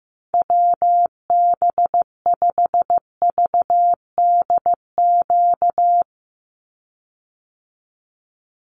Morse code, heard as a keyed tone, about 15 wpm, 700 Hz. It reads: WB5VDQ
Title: USA Call Signs 15wpm